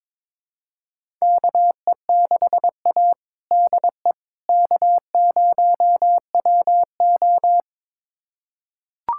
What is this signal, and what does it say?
Morse code, heard as a keyed tone, about 22 wpm, 700 Hz. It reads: KE6A DE K0WO
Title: USA Callsign DE Callsign - 4 Characters - 22wpm